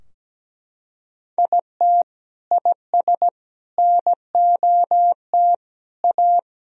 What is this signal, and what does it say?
Morse code, heard as a keyed tone, about 17 wpm, 700 Hz. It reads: IT IS NOT A
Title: Sentences from Top 1000 Words - Review Part 4 - 17wpm